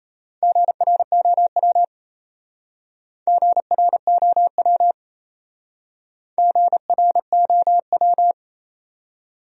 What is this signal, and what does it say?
Morse code, heard as a keyed tone, about 32 wpm, 700 Hz. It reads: GROW GROW GROW
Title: Speed-Racing Top 300 Words 28wpm